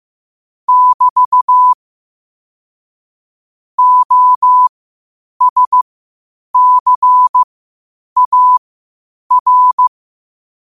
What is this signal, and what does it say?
Morse code, heard as a keyed tone, about 15 wpm, spaced as for 8 wpm, 1000 Hz. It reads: <BT> OSCAR
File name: qotdfiles / 8wpm